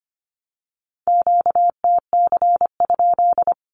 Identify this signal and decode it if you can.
Morse code, heard as a keyed tone, about 25 wpm, 700 Hz. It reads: QTC?
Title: Q-Codes 25wpm